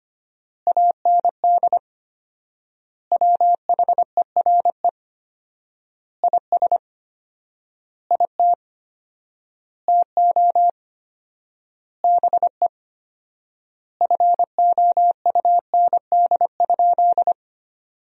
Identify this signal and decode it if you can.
Morse code, heard as a keyed tone, about 25 wpm, 700 Hz. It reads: AND WHERE IS IT TO BE FOUND?
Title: 4x Word-Spacing - Sentences from Top 300 Words - 25wpm